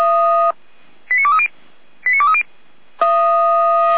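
Five-tone selective call (ZVEI), heard as quick runs of five tones, tones of about 70 ms. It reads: 89230 89230
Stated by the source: Five-tone ZVEI selective call, received with an SDR.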